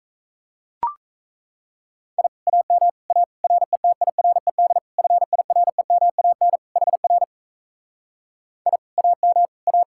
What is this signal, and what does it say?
Morse code, heard as a keyed tone, about 38 wpm, 700 Hz. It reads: I AM A RETIRED FIREMAN HR I AM A